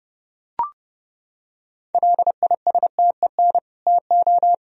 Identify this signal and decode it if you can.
Morse code, heard as a keyed tone, about 30 wpm, 700 Hz. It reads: LISTEN TO
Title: Sentences from Top 400 Words and Abbreviations 30wpm